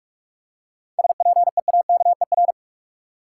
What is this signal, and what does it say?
Morse code, heard as a keyed tone, about 46 wpm, 700 Hz. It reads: SPEAKER
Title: Seven Letter Words - 45wpm